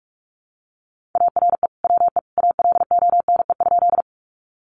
Morse code, heard as a keyed tone, about 44 wpm, 700 Hz. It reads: ARE WE ALONE?
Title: Sentences from Top 1000 Words 45wpm